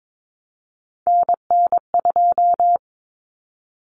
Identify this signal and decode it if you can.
Morse code, heard as a keyed tone, about 22 wpm, 700 Hz. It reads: NN2
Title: Common Cut Numbers - 3-Digit 22wpm